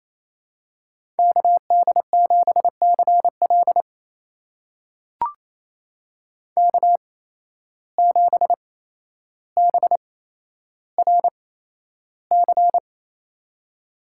Morse code, heard as a keyed tone, about 27 wpm, spaced as for 12 wpm, 700 Hz. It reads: KD7CL K7BRC
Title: USA Call Signs - 5 Characters - 8x Character Spacing - 28wpm